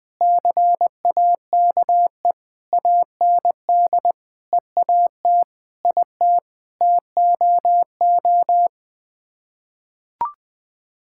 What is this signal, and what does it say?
Morse code, heard as a keyed tone, about 21 wpm, 700 Hz. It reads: CAKE AND EAT IT TOO
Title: Sets of 9 Words - 20wpm